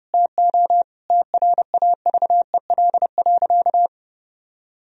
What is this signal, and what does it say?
Morse code, heard as a keyed tone, about 29 wpm, 700 Hz. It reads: TO TRAVEL.
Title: Sentences from Top 1000 Words - Review Part 2 - 30wpm